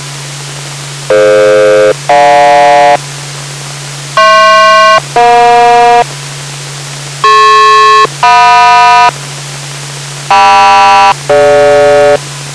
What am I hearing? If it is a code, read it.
Aircraft SELCAL, heard as two pairs of tones. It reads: DF-GJ HP-EJ DP-JM KM-DG